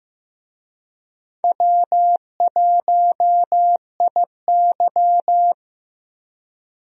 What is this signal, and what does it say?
Morse code, heard as a keyed tone, about 15 wpm, 700 Hz. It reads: W1IY
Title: USA Call Signs - 4 Characters - Repeated 2x - 15wpm